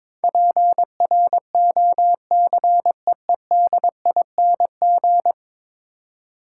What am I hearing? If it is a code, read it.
Morse code, heard as a keyed tone, about 22 wpm, 700 Hz. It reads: PROCEEDING
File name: practice / 22wpm / 10 Letter Words - First 3 Letters Given - 22wpm